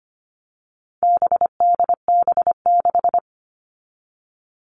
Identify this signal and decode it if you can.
Morse code, heard as a keyed tone, about 25 wpm, 700 Hz. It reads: BDB6
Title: All Cut Numbers - 4-Digit 25wpm